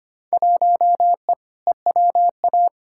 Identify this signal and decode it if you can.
Morse code, heard as a keyed tone, about 25 wpm, 700 Hz. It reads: 1E EWA
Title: Field Day Exchanges - 25wpm